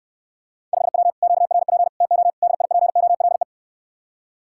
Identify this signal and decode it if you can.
Morse code, heard as a keyed tone, about 69 wpm, 690 Hz. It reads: 5W QRP TO DIPOLE